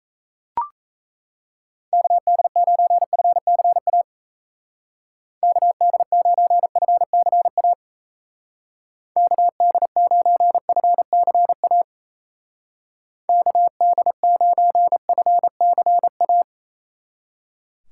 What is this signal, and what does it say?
Morse code, heard as a keyed tone, about 34 wpm, 700 Hz. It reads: KD9FCA KD9FCA KD9FCA KD9FCA
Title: Speed Racing - USA Call Signs 28wpm